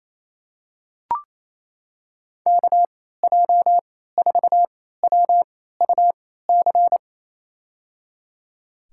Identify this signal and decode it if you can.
Morse code, heard as a keyed tone, about 28 wpm, spaced as for 16 wpm, 700 Hz. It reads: KJ4WUC
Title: USA Call Signs - 6 Characters - 3x Character Spacing - 28wpm